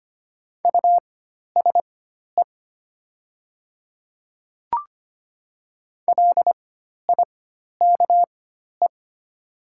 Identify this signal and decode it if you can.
Morse code, heard as a keyed tone, about 25 wpm, spaced as for 12 wpm, 700 Hz. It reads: USE LIKE
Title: 4x Character Spacing - Top 100 Words - 25wpm